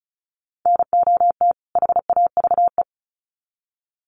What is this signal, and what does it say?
Morse code, heard as a keyed tone, about 34 wpm, 700 Hz. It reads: NOT HAVE
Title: Sets of 2 Words 35wpm